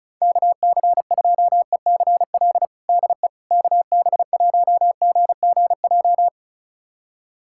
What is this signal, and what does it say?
Morse code, heard as a keyed tone, about 36 wpm, 700 Hz. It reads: KC2ECL DE KB1GGJ
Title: USA Callsign DE Callsign - 6 Characters - 35wpm